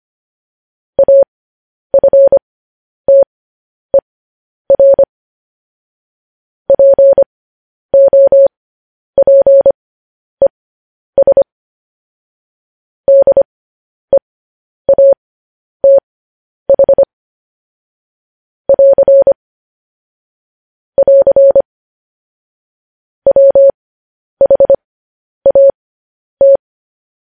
Morse code, heard as a keyed tone, about 25 wpm, spaced as for 10 wpm, 550 Hz. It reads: AFTER POPES DEATH <AR> <AR> WHAT